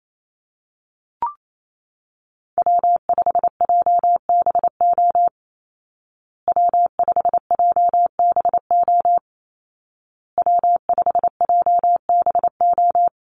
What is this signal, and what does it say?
Morse code, heard as a keyed tone, about 28 wpm, 700 Hz. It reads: W5JBO W5JBO W5JBO
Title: Call Signs Repeated 2x - 28wpm